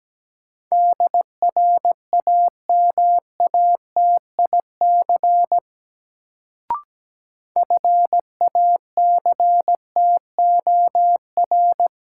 Morse code, heard as a keyed tone, about 17 wpm, 700 Hz. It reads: DRAMATIC FACTOR
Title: Sentences from Top 1300 Words - New Word Drill Alt - 17wpm